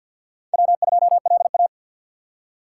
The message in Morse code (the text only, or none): W2LA